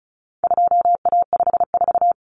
Morse code, heard as a keyed tone, about 34 wpm, 700 Hz. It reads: 2A54
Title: Common Cut Numbers - 4-Digit 35wpm